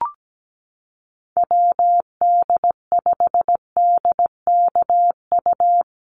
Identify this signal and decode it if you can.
Morse code, heard as a keyed tone, about 17 wpm, 700 Hz. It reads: WD5DKU
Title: USA Call Signs Repeated 3x 17wpm